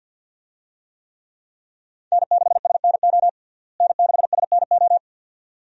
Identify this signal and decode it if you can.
Morse code, heard as a keyed tone, about 50 wpm, 700 Hz. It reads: N6SNO N6SNO
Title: USA Call Signs - 5 Characters - Repeated 2x - 50wpm